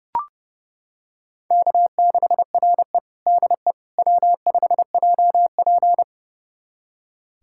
Morse code, heard as a keyed tone, about 30 wpm, 700 Hz. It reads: K6RE DE W5JP
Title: USA Callsign DE Callsign - 4 Characters - 30wpm